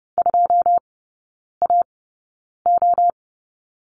Morse code, heard as a keyed tone, about 30 wpm, spaced as for 9 wpm, 700 Hz. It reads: 2AO
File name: course / 30:10wpm / 087 Lesson 15e - Number 5 - Three Characters 30wpm